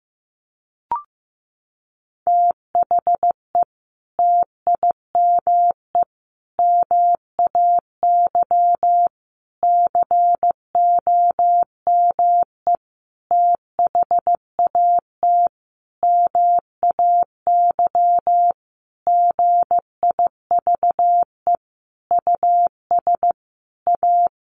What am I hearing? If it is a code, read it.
Morse code, heard as a keyed tone, about 15 wpm, 700 Hz. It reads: THE TIME MAY COME THAT MAY GIVE US A
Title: Sentences from Top 1100 Words 15wpm